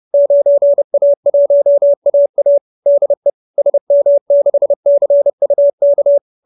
Morse code, heard as a keyed tone, about 30 wpm, 570 Hz. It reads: 9A1AA DE SM6CUK
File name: QSO319 30